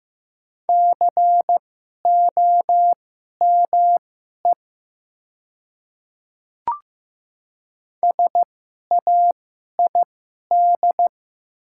Morse code, heard as a keyed tone, about 15 wpm, spaced as for 11 wpm, 700 Hz. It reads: COME SAID
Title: 2x Character Spacing - Top 200 Words - 15wpm